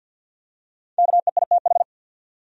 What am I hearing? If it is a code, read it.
Morse code, heard as a keyed tone, about 46 wpm, 700 Hz. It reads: KEITH
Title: USA Common Names - 50wpm